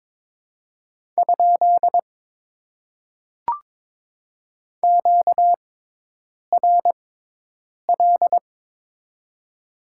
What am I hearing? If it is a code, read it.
Morse code, heard as a keyed tone, about 22 wpm, spaced as for 8 wpm, 700 Hz. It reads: ? QRL
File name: Top 90 QSO Elements - 6x Character Spacing - 22wpm